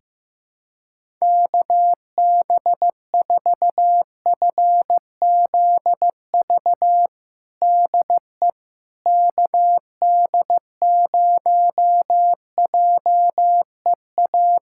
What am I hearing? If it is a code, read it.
Morse code, heard as a keyed tone, about 15 wpm, 700 Hz. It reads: KB4FZV DE KD0JEA
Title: USA Callsign DE Callsign - 6 Characters - 15wpm